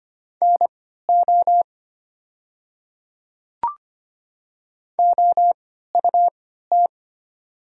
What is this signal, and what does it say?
Morse code, heard as a keyed tone, about 25 wpm, spaced as for 14 wpm, 700 Hz. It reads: NO OUT
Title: 3x Character Spacing - Top 100 Words - 25wpm